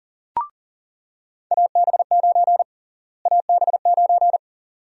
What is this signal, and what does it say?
Morse code, heard as a keyed tone, about 40 wpm, 700 Hz. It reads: AB9 AB9
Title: USA Call Sign Prefixes - Repeated 2x - 40wpm